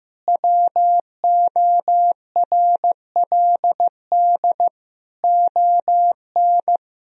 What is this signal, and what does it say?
Morse code, heard as a keyed tone, about 15 wpm, 700 Hz. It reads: WORLD ON
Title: Sentences from Top 1000 Words - Review Part 2 - 15wpm